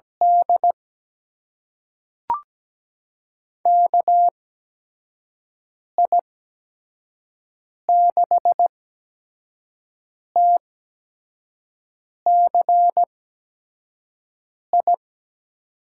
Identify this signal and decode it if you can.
Morse code, heard as a keyed tone, about 17 wpm, spaced as for 5 wpm, 700 Hz. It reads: D KI6TCI
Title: USA Call Signs - 8x Character Spacing - 17wpm